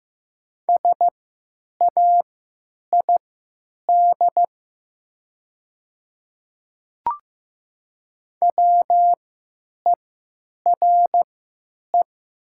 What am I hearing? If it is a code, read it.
Morse code, heard as a keyed tone, about 15 wpm, spaced as for 8 wpm, 700 Hz. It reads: SAID WERE
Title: 3x Character Spacing - Top 100 Words - 15wpm